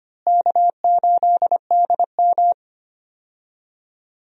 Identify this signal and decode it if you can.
Morse code, heard as a keyed tone, about 25 wpm, 700 Hz. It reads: K8DM
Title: USA Call Signs - 4 Characters - Repeated 2x - 25wpm